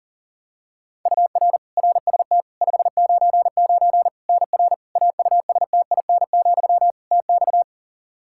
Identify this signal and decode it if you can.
Morse code, heard as a keyed tone, about 40 wpm, 700 Hz. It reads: UR RST 599 NR AUSTIN, TX
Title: Practice QSO - Exchange 1 - Beginning - 40wpm